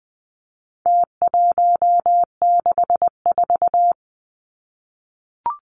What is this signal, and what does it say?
Morse code, heard as a keyed tone, about 20 wpm, 700 Hz. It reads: T164
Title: Numbers - Contest Style - 4-Digit 20wpm